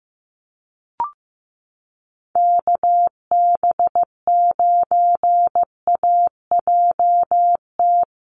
Morse code, heard as a keyed tone, about 14 wpm, 700 Hz. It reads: KB9AJT
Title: USA Call Signs Repeated 3x 15wpm